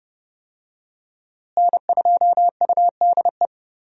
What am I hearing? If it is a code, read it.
Morse code, heard as a keyed tone, about 30 wpm, 700 Hz. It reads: N2UDE
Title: USA Call Signs - 5 Characters - Repeated 3x - 30wpm